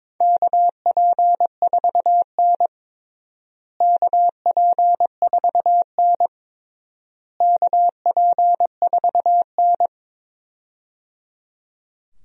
Morse code, heard as a keyed tone, about 22 wpm, 700 Hz. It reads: KP4N KP4N KP4N
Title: DXCC-100 Call Signs - 4 Characters - Repeated 3x - 22wpm